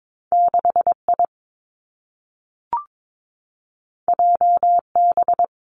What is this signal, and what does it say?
Morse code, heard as a keyed tone, about 22 wpm, 700 Hz. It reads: 6 I J B
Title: Two Challenging Letter-Number Combinations 22wpm